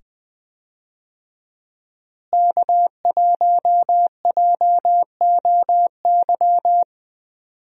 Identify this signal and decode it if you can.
Morse code, heard as a keyed tone, about 20 wpm, 700 Hz. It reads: K1JOY
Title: USA Call Signs - 5 Characters - Repeated 2x - 20wpm